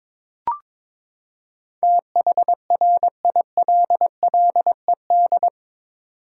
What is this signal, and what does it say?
Morse code, heard as a keyed tone, about 21 wpm, 700 Hz. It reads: THRILLED
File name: practice / 22wpm / Eight Letter Words - 22wpm